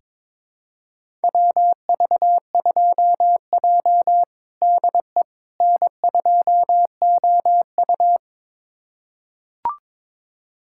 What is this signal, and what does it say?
Morse code, heard as a keyed tone, about 22 wpm, 700 Hz. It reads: WV2J DE N2OU
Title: USA Callsign DE Callsign - 4 Characters - 22wpm